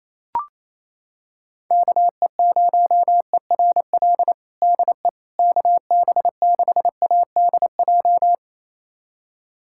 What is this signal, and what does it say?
Morse code, heard as a keyed tone, about 28 wpm, 700 Hz. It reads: KE0ERL DE KB6ADJ
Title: USA Callsign DE Callsign 28wpm